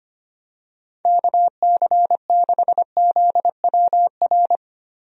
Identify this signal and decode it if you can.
Morse code, heard as a keyed tone, about 25 wpm, 700 Hz. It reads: KC6ZWR
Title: USA Call Signs - 6 Characters - Repeated 3x - 25wpm